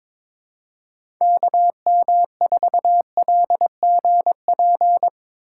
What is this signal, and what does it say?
Morse code, heard as a keyed tone, about 22 wpm, 700 Hz. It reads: KM4LGP
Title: USA Call Signs - 4x Character Spacing - 22wpm